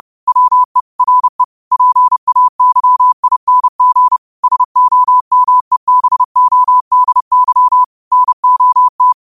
Morse code, heard as a keyed tone, about 30 wpm, 1000 Hz. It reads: WE RE PAYING SOMEBODY NOT
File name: qotdfiles / 30wpm